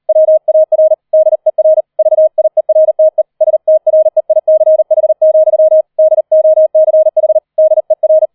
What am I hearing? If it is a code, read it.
Morse code, heard as a keyed tone, about 38 wpm, 630 Hz. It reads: WAR DER VIERTE STREICH, DOCH DER
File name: 004 Vierter Streich 40wpm